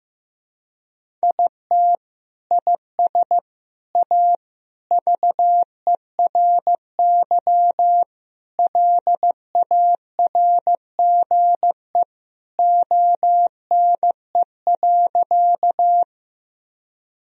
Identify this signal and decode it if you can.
Morse code, heard as a keyed tone, about 15 wpm, 700 Hz. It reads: IT IS A VERY LARGE ONE.
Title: Sentences from Top 1000 Words - Review Part 1 - 15wpm